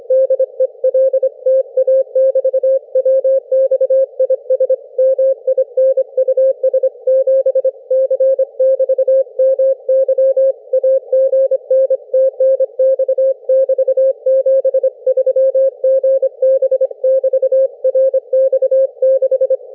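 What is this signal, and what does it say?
Morse code, heard as a keyed tone, about 25 wpm, 530 Hz. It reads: DELTA <BT> WX IS MINUS 7 C <BT> MY AGN TNX <BT> 73 GB <BT> RX6
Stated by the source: CW QSO between UA1CEG and RX6MY on homebrew rigs, three watts QRP into a delta antenna in forest tree branches, heard thousands of kilometres away; RU-QRP Club.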